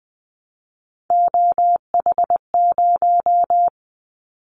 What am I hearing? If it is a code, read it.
Morse code, heard as a keyed tone, about 20 wpm, 700 Hz. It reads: OH0
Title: Callsign Prefixes for Top 100 DXCC - 20wpm